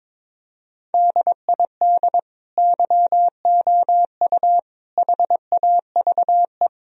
Morse code, heard as a keyed tone, about 22 wpm, 700 Hz. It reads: DID YOU HAVE
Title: Sentences from Top 1000 Words - Review Part 3 - 22wpm